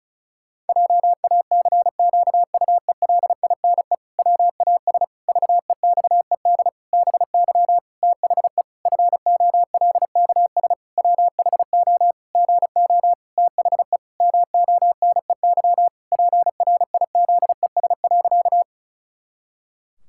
Morse code, heard as a keyed tone, about 34 wpm, 700 Hz. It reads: JACQUELINE WAS VEXED BY THE FOLKS WHO GO THE MONEY PRIZES.
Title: Pangram Challenge v2 - 35wpm